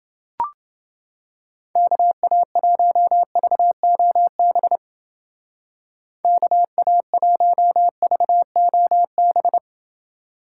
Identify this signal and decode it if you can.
Morse code, heard as a keyed tone, about 28 wpm, 700 Hz. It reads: KA1VOB KA1VOB